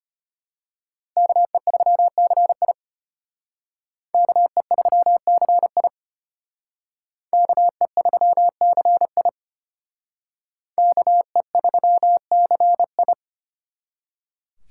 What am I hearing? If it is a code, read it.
Morse code, heard as a keyed tone, about 32 wpm, 700 Hz. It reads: KE3CI KE3CI KE3CI KE3CI